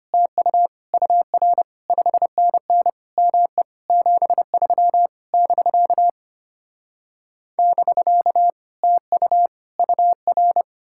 Morse code, heard as a keyed tone, about 33 wpm, 700 Hz. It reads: TU UR 5NN ME 73 <BK> <BK> TU UR